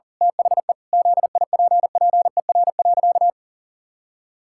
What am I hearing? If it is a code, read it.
Morse code, heard as a keyed tone, about 40 wpm, 700 Hz. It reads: THE ZIPPER.
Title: Pangram Challenge v2 - 40wpm